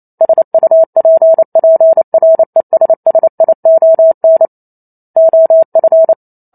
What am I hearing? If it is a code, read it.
Morse code, heard as a keyed tone, about 28 wpm, 650 Hz. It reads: SUPPRESSION OF